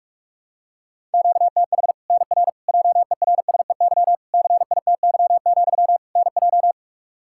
Morse code, heard as a keyed tone, about 46 wpm, 700 Hz. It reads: QTH NR JERSEY CITY, NJ
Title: K1USN SST QSO Practice 45wpm